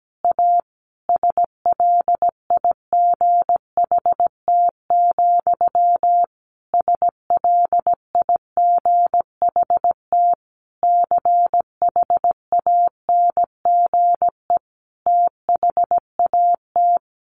Morse code, heard as a keyed tone, about 17 wpm, 700 Hz. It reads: A SLIGHT, SLIGHT CHANGE THAT
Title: A Sound Of Thunder 17wpm